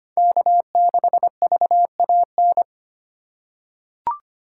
Morse code, heard as a keyed tone, about 25 wpm, 700 Hz. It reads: K6VAN
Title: USA Call Signs - 5 Characters - 3x Character Spacing - 25wpm